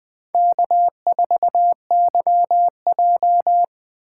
Morse code, heard as a keyed tone, about 20 wpm, 700 Hz. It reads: K4YJ
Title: USA Call Signs - 4 Characters - 2x Character Spacing - 20wpm